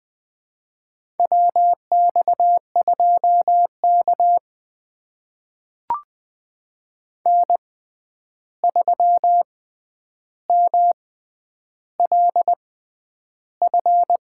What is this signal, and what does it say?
Morse code, heard as a keyed tone, about 20 wpm, spaced as for 10 wpm, 700 Hz. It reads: WX2K N3MLF
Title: USA Call Signs - 6x Character Spacing - 20wpm